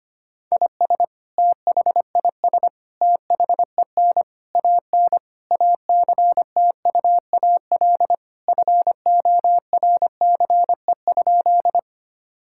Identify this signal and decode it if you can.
Morse code, heard as a keyed tone, about 25 wpm, 700 Hz. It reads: IS THIS THEN AN ACTUAL FORCE?
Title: Sentences from Top 1000 Words - Review Part 5 - 25wpm